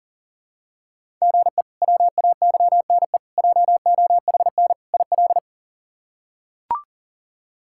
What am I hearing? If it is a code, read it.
Morse code, heard as a keyed tone, about 40 wpm, 700 Hz. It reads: GE WAYNE JOHN IL
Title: K1USN SST QSO Practice - 40wpm